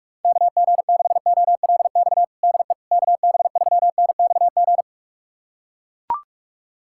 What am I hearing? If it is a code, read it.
Morse code, heard as a keyed tone, about 41 wpm, 700 Hz. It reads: KG6OLX DE KB3NXG
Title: USA Callsign DE Callsign - 6 Characters - 45wpm